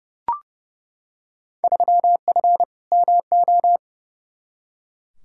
Morse code, heard as a keyed tone, about 30 wpm, 700 Hz. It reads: 3F MO
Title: Field Day Exchanges Encore - 30wpm